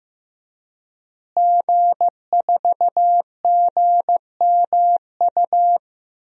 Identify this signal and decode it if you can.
Morse code, heard as a keyed tone, about 15 wpm, 700 Hz. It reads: G4GMU
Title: DXCC-100 Call Signs - 5 Characters - Repeated 3x - 15wpm